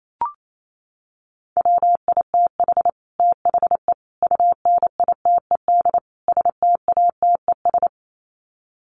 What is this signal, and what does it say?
Morse code, heard as a keyed tone, about 28 wpm, 700 Hz. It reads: WITH THE UNITED STATES
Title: Sets of 4 Words 28wpm